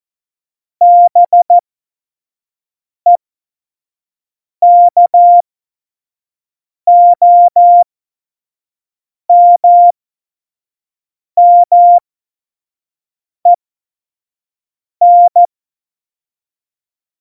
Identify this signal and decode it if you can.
Morse code, heard as a keyed tone, about 14 wpm, spaced as for 5 wpm, 700 Hz. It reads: BEKOMMEN
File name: Morsekurs / Witze / Witz 30. Januar 2018, 25 F70